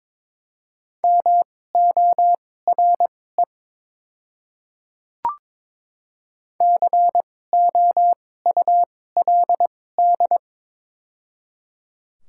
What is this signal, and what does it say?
Morse code, heard as a keyed tone, about 22 wpm, spaced as for 16 wpm, 700 Hz. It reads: MORE COULD
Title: 2x Character Spacing - Top 100 Words - 22wpm